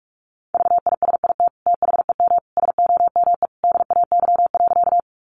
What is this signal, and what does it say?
Morse code, heard as a keyed tone, about 44 wpm, 700 Hz. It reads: VISIT THEM SOME DAY.
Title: Sentences from Top 900 Words 45wpm